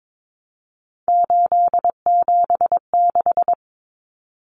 Morse code, heard as a keyed tone, about 22 wpm, 700 Hz. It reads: 876
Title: Numbers - 3-Digit 22wpm